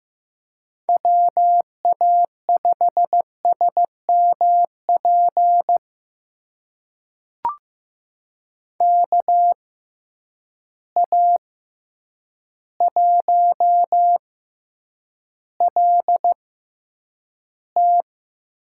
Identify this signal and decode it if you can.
Morse code, heard as a keyed tone, about 15 wpm, spaced as for 8 wpm, 700 Hz. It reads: WA5SMP KA1LT
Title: USA Call Signs - 6 Characters - 6x Character Spacing - 15wpm